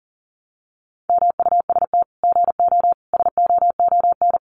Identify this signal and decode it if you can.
Morse code, heard as a keyed tone, about 40 wpm, 700 Hz. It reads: MUST GO SOON
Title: Common QSO Phrases 40wpm